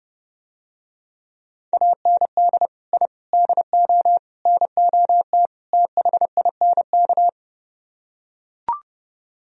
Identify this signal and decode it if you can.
Morse code, heard as a keyed tone, about 30 wpm, 700 Hz. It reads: AND I DO NOT THINK